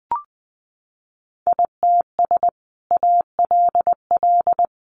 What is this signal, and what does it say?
Morse code, heard as a keyed tone, about 20 wpm, 700 Hz. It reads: ITS ALL
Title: Sentences from Top 600 Words 20wpm